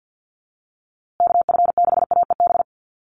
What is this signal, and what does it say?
Morse code, heard as a keyed tone, about 50 wpm, 700 Hz. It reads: KF6AEB
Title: USA Call Signs Repeated 3x 50wpm